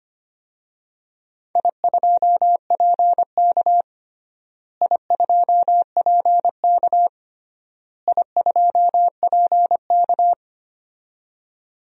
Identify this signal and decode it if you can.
Morse code, heard as a keyed tone, about 25 wpm, 700 Hz. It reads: I2PK I2PK I2PK
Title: DXCC-100 Call Signs - 4 Characters - Repeated 3x - 25wpm